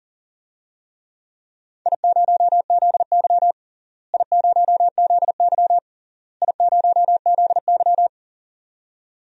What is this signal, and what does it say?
Morse code, heard as a keyed tone, about 40 wpm, 700 Hz. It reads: I0ZY I0ZY I0ZY
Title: DXCC-100 Call Signs - 4 Characters - Repeated 3x - 40wpm